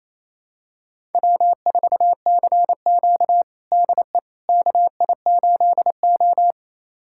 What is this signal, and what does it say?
Morse code, heard as a keyed tone, about 28 wpm, 700 Hz. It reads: W4CQ DE KI8O
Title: USA Callsign DE Callsign - 4 Characters - 28wpm